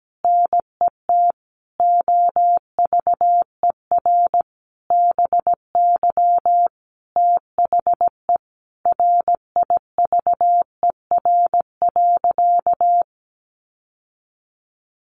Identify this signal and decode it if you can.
Morse code, heard as a keyed tone, about 17 wpm, 700 Hz. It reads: NET OVER BY THE RIVER.
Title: Sentences from Top 1400 Words 17wpm